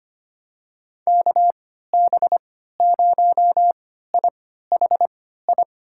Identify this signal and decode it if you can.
Morse code, heard as a keyed tone, about 25 wpm, spaced as for 14 wpm, 700 Hz. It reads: KB0IHI
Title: USA Call Signs - 6 Characters - 3x Character Spacing - 25wpm